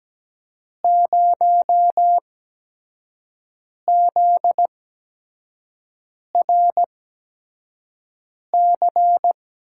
Morse code, heard as a keyed tone, about 17 wpm, spaced as for 5 wpm, 700 Hz. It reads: 0ZRC
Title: USA Call Signs - 8x Character Spacing - 17wpm